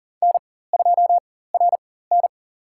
Morse code, heard as a keyed tone, about 38 wpm, spaced as for 18 wpm, 700 Hz. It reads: N2RN
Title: USA Call Signs - 4 Characters - 4x Character Spacing - 40wpm